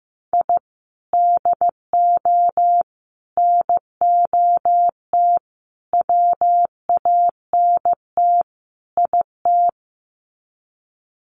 Morse code, heard as a keyed tone, about 15 wpm, 700 Hz. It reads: I DO NOT WANT IT
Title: Sets of 5 Words Encore2 15wpm